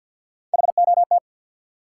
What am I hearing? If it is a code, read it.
Morse code, heard as a keyed tone, about 47 wpm, 700 Hz. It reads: HOT
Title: Sentences from Top 500 Words - New Word Drill - 50wpm